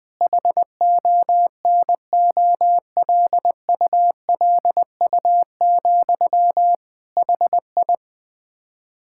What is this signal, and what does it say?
Morse code, heard as a keyed tone, about 21 wpm, 700 Hz. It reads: HONOLULU, HI
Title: Practice QSO - Exchange 1 - Beginning - 20wpm